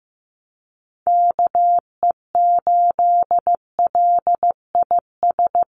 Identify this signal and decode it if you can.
Morse code, heard as a keyed tone, about 15 wpm, 700 Hz. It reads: KE8LIS
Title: USA Call Signs 15wpm